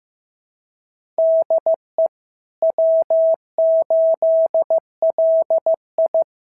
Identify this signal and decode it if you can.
Morse code, heard as a keyed tone, about 14 wpm, 650 Hz. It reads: DE W8LI
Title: SOTA QSO Practice 15wpm